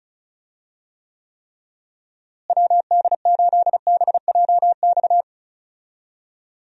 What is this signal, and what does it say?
Morse code, heard as a keyed tone, about 36 wpm, 700 Hz. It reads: WD8BJX
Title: USA Call Signs - 6 Characters - Repeated 3x - 35wpm